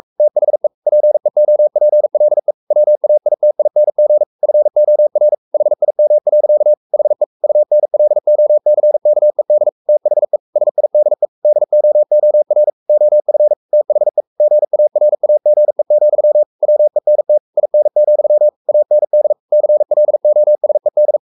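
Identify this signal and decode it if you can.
Morse code, heard as a keyed tone, about 44 wpm, 600 Hz. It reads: THE PEOPLE WAITING FOR HIM. HE UNLOCKED THE SIDE DOOR OF THE GARAGE, WENT IN, AND CLOSED
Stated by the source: Morse code at 43 wpm sending Ken Follett's novel Never in English, from CW med Gnister.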